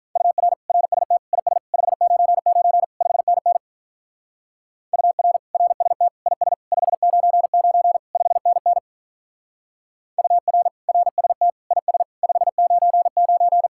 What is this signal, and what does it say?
Morse code, heard as a keyed tone, about 44 wpm, 700 Hz. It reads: UR RST IS 599 5NN UR RST IS 599 5NN UR RST IS 599